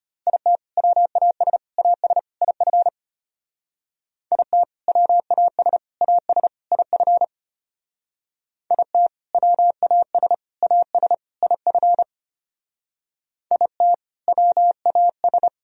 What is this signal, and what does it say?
Morse code, heard as a keyed tone, about 31 wpm, 700 Hz. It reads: IT WAS AS IF IT WAS AS IF IT WAS AS IF IT WAS
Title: Speed-Racing - Sets of 4 Words 25wpm